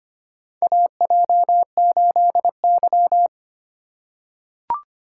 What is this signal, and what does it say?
Morse code, heard as a keyed tone, about 25 wpm, 700 Hz. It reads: AJ8Y
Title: USA Call Signs - 4 Characters - 3x Character Spacing - 25wpm